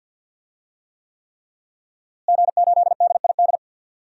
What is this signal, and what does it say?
Morse code, heard as a keyed tone, about 50 wpm, 700 Hz. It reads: G8DID
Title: DXCC-100 Call Signs - 5 Characters - Repeated 3x - 50wpm